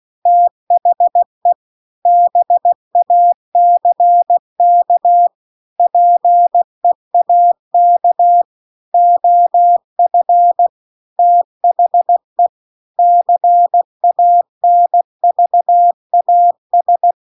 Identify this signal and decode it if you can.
Morse code, heard as a keyed tone, about 16 wpm, 700 Hz. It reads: THE BACK PEAK OF THE CANVAS